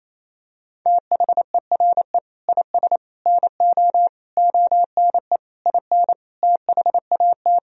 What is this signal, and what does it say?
Morse code, heard as a keyed tone, about 28 wpm, 700 Hz. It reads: THERE IS NO ONE IN THAT
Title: Sentences from Top 500 Words and Abbreviations 28wpm